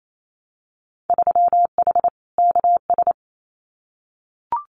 Morse code, heard as a keyed tone, about 28 wpm, 700 Hz. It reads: 3H KS
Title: Winter Field Day Exchanges 28wpm